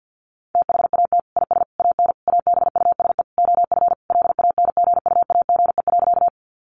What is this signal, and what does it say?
Morse code, heard as a keyed tone, about 50 wpm, 700 Hz. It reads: THAT IS AN ABUSE OF LANGUAGE.
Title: Sentences from Top 1300 Words 50wpm